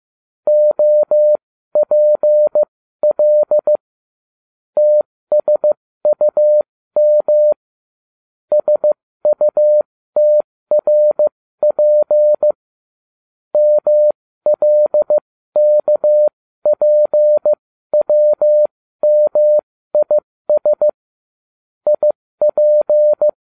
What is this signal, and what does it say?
Morse code, heard as a keyed tone, about 15 wpm, spaced as for 13 wpm, 600 Hz. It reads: OPL TSUM SUTRP MLKPWMIS IP